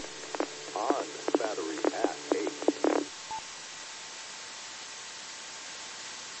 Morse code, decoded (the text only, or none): none